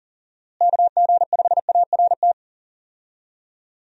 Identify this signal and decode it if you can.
Morse code, heard as a keyed tone, about 40 wpm, 700 Hz. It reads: KG5ART
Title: USA Call Signs - 6 Characters - Repeated 2x - 40wpm